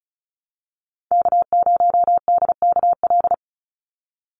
Morse code, heard as a keyed tone, about 34 wpm, 700 Hz. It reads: K0DKL
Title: USA Call Signs 35wpm